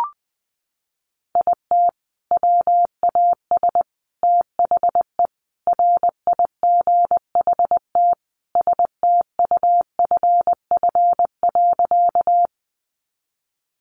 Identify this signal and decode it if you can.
Morse code, heard as a keyed tone, about 20 wpm, 700 Hz. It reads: IT WAS THE RIGHT STUFF.
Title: Sentences from Top 1400 Words 20wpm